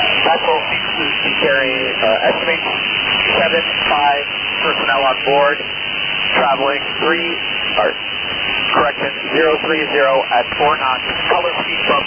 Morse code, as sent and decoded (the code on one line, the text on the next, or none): none